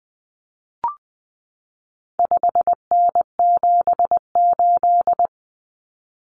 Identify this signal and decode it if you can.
Morse code, heard as a keyed tone, about 20 wpm, 700 Hz. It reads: 5N78
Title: Common Cut Numbers - 4-Digit 20wpm